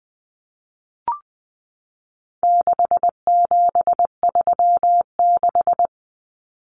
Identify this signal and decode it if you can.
Morse code, heard as a keyed tone, about 20 wpm, 700 Hz. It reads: 6736
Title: Common Cut Numbers - 4-Digit 20wpm